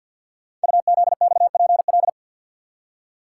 Morse code, heard as a keyed tone, about 52 wpm, 700 Hz. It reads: UZXPL
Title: Code Groups - 5 Letters - 50wpm